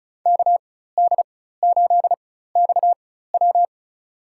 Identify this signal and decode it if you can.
Morse code, heard as a keyed tone, about 36 wpm, spaced as for 16 wpm, 700 Hz. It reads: KD8XW
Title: USA Call Signs - 5 Characters - 4x Character Spacing - 35wpm